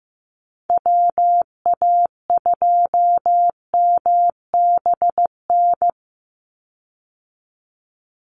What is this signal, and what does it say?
Morse code, heard as a keyed tone, about 15 wpm, 700 Hz. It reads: WA2MBN
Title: USA Call Signs 15wpm